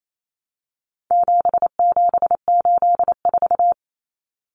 Morse code, heard as a keyed tone, about 28 wpm, 700 Hz. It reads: 7784
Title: Common Cut Numbers - 4-Digit 28wpm